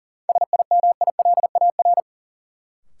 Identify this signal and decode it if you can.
Morse code, heard as a keyed tone, about 40 wpm, 700 Hz. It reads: SIMILAR
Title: Sentences from Top 1000 Words - Review Part 5 - New Word Drill - 40wpm